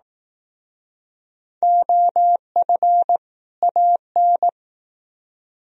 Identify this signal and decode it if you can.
Morse code, heard as a keyed tone, about 19 wpm, 700 Hz. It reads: OF AN